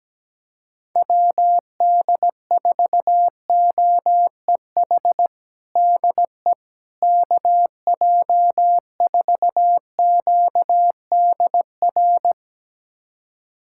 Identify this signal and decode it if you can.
Morse code, heard as a keyed tone, about 17 wpm, 700 Hz. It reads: WD4OEH DE KJ4QDR
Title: USA Callsign DE Callsign - 6 Characters - 17wpm